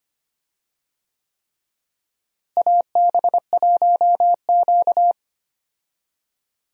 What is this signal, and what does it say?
Morse code, heard as a keyed tone, about 25 wpm, 700 Hz. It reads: AB1Q
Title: USA Call Signs - 4 Characters - Repeated 3x - 25wpm